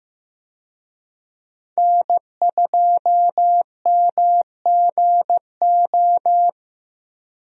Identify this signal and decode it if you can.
Morse code, heard as a keyed tone, about 15 wpm, 700 Hz. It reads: N2MGO
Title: USA Call Signs - 5 Characters - 15wpm